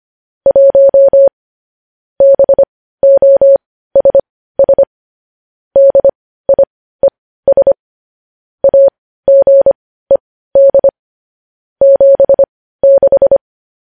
Morse code, heard as a keyed tone, about 25 wpm, spaced as for 15 wpm, 550 Hz. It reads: 1 BOSS DIES AGED 76
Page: news-headlines-in-morse-code-at-15-wpm